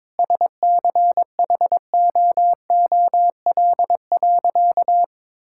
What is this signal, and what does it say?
Morse code, heard as a keyed tone, about 22 wpm, 700 Hz. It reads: SCHOOL.
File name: Difficult Sentences with Hints - 22wpm